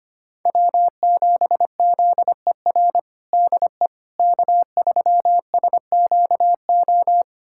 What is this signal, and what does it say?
Morse code, heard as a keyed tone, about 25 wpm, 700 Hz. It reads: W7ZER DE K3SQO
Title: USA Callsign DE Callsign - 5 Characters - 25wpm